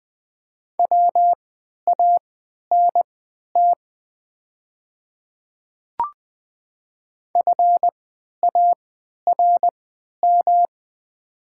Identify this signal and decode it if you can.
Morse code, heard as a keyed tone, about 20 wpm, spaced as for 11 wpm, 700 Hz. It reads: WANT FARM
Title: 3x Character Spacing - Top 500 Words - 20wpm